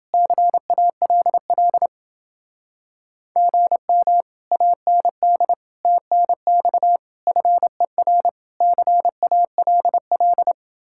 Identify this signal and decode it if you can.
Morse code, heard as a keyed tone, about 28 wpm, 700 Hz. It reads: CALL GM AND TNX FER CALL